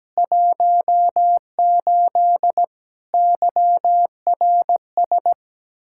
Morse code, heard as a keyed tone, about 17 wpm, 700 Hz. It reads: 18 YRS
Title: Practice QSO - Exchange 3 - More Info - 17wpm